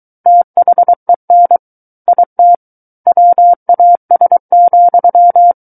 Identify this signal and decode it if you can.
Morse code, heard as a keyed tone, about 23 wpm, 700 Hz. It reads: THEN IT WAS,
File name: Book One / 14-Chapter 14 - 23 WPM